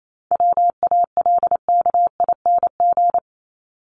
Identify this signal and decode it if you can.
Morse code, heard as a keyed tone, about 28 wpm, 700 Hz. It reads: WALKING
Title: ING Suffix - Top 100 Words with Base Words 28wpm